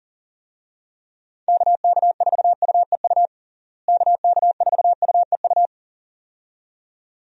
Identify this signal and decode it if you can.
Morse code, heard as a keyed tone, about 40 wpm, 700 Hz. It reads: KK4UEU KK4UEU
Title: USA Call Signs - 6 Characters - Repeated 2x - 40wpm